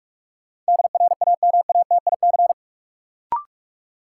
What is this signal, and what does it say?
Morse code, heard as a keyed tone, about 46 wpm, 700 Hz. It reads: DRAMATIC
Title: Sentences from Top 1300 Words - New Word Drill - 45wpm